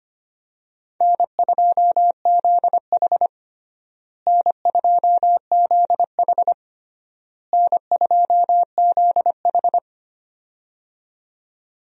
Morse code, heard as a keyed tone, about 25 wpm, 700 Hz. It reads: N2ZH N2ZH N2ZH
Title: USA Call Signs - 4 Characters - Repeated 3x - 25wpm